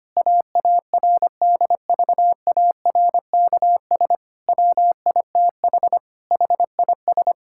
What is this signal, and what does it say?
Morse code, heard as a keyed tone, about 25 wpm, 700 Hz. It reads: AARDVARKS WITH HIS
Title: Pangram Challenge v2 - 25wpm